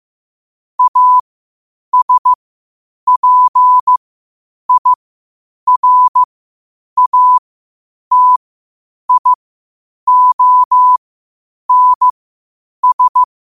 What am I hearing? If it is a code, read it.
Morse code, heard as a keyed tone, about 15 wpm, spaced as for 8 wpm, 1000 Hz. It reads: ASPIRATIONS